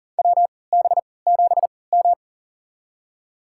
Morse code, heard as a keyed tone, about 40 wpm, spaced as for 23 wpm, 700 Hz. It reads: WB7M
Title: USA Call Signs - 3x Character Spacing - 40wpm